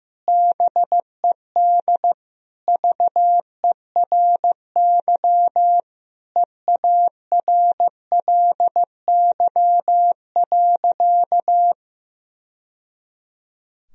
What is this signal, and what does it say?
Morse code, heard as a keyed tone, about 16 wpm, 700 Hz. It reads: BED VERY EARLY.
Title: Sentences from Top 1000 Words - Review Part 2 - 15wpm